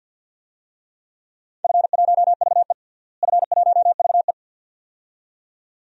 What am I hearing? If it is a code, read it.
Morse code, heard as a keyed tone, about 52 wpm, 700 Hz. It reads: F1VE F1VE
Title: DXCC-100 Call Signs - 4 Characters - Repeated 2x - 50wpm